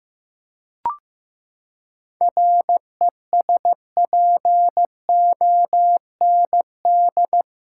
Morse code, heard as a keyed tone, about 15 wpm, 700 Hz. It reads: RESPOND
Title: Seven Letter Words - 15wpm